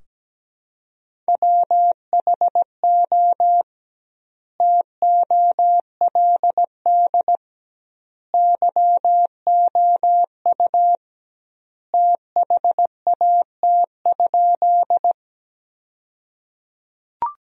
Morse code, heard as a keyed tone, about 17 wpm, 700 Hz. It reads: WHO TOLD YOU THAT?
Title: 2x Word-Spacing - Sentences from Top 400 Words - 17wpm